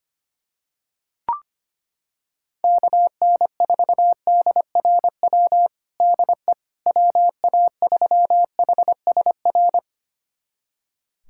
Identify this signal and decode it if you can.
Morse code, heard as a keyed tone, about 25 wpm, 700 Hz. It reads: KN4DRW DE WA3HSR
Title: USA Callsign DE Callsign 25wpm